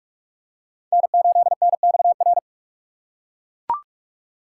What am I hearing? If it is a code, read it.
Morse code, heard as a keyed tone, about 43 wpm, 700 Hz. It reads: N8NXR
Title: USA Call Signs - 5 Characters - 4x Character Spacing - 45wpm